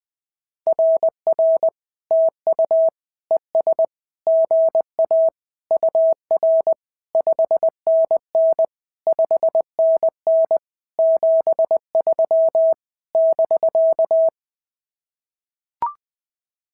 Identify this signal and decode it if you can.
Morse code, heard as a keyed tone, about 19 wpm, 650 Hz. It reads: RR TU ES GA UR 5NN 5NN 73 <BK>
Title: SOTA QSO Practice 20wpm